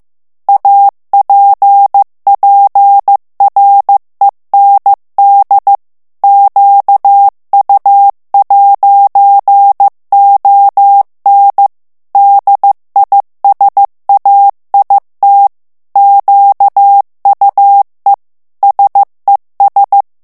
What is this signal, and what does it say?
Morse code, heard as a keyed tone, about 15 wpm, 800 Hz. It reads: APPREND QU'ON DISAIT QUE SES
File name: FRANCAIS TEXTE CLAIR 2 VITESSE 14 MOTS MINUTES